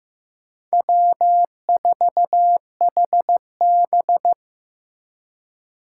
Morse code, heard as a keyed tone, about 18 wpm, 700 Hz. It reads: W4HB
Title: Speed Racing - USA 1x2 Call Signs 15wpm